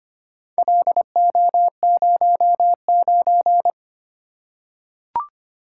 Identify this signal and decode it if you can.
Morse code, heard as a keyed tone, about 25 wpm, 700 Hz. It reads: LO09
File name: practice / 25wpm / Four Letter-Number Combinations - 25wpm